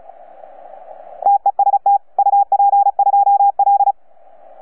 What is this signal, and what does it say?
Morse code, heard as a keyed tone, about 34 wpm, 800 Hz. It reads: TEST UP2L
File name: UP2L-20-CW